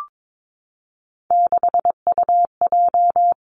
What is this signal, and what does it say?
Morse code, heard as a keyed tone, about 22 wpm, 700 Hz. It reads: 6UJ
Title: Three Challenging Letter-Number Combinations 22wpm